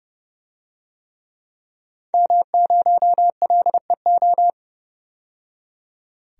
Morse code, heard as a keyed tone, about 30 wpm, 700 Hz. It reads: M0LEO
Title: DXCC-100 Call Signs - 5 Characters - 30wpm